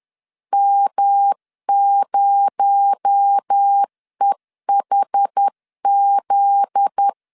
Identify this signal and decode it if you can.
Morse code, heard as a keyed tone, about 10 wpm, 800 Hz. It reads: M0EHZ
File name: LEZIONI / 9 286 10wpm 2735